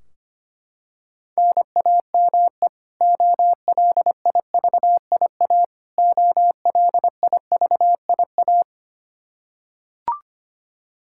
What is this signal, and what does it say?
Morse code, heard as a keyed tone, about 25 wpm, 700 Hz. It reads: NAME OLIVIA OLIVIA
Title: Practice QSO - Exchange 1 - Beginning - 25wpm